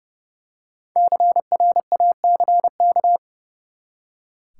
Morse code, heard as a keyed tone, about 30 wpm, 700 Hz. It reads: CRACK
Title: Five Letter Words - 30wpm